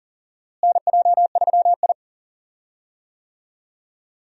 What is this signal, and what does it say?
Morse code, heard as a keyed tone, about 39 wpm, 700 Hz. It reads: NJ3I
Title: USA Call Signs - 4 Characters - Repeated 2x - 40wpm